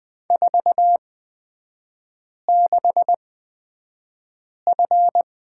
Morse code, heard as a keyed tone, about 20 wpm, spaced as for 5 wpm, 700 Hz. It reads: 46F
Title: 236 Lesson 38e - Letter q - Three Characters 20wpm